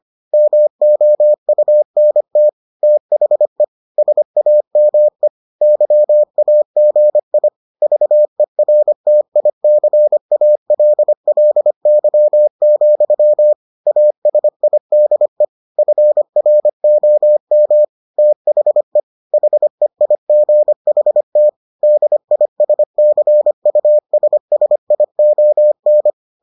Morse code, heard as a keyed tone, about 25 wpm, 600 Hz. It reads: MOUNT THE SAME YAGI VERTICALLY, ASIDE FROM THE HEIGHT DISCUSSION